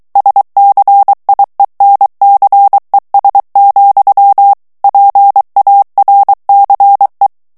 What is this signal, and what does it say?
Morse code, heard as a keyed tone, about 23 wpm, 800 Hz. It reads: SCIENCES, PARCE
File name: morse mp3 / FRANCAIS TEXTE CLAIR 1 VITESSE 22 MOTS MINUTES